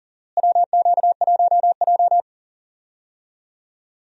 Morse code, heard as a keyed tone, about 37 wpm, 700 Hz. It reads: WQ1J
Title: USA Call Signs - 4 Characters - Repeated 3x - 40wpm